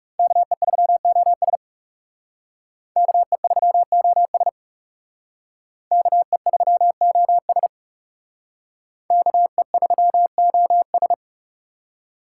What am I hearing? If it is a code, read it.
Morse code, heard as a keyed tone, about 34 wpm, 700 Hz. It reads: KE3OS KE3OS KE3OS KE3OS
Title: Speed Racing - USA 2x2 Call Signs 30wpm